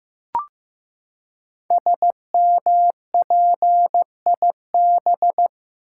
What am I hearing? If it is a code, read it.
Morse code, heard as a keyed tone, about 15 wpm, 700 Hz. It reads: SMPIB
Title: Code Groups - 5 Letters - 15wpm